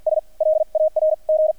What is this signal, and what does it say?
Morse code, heard as a keyed tone, about 48 wpm, spaced as for 39 wpm, 620 Hz. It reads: S LARG
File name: archived / CRSnet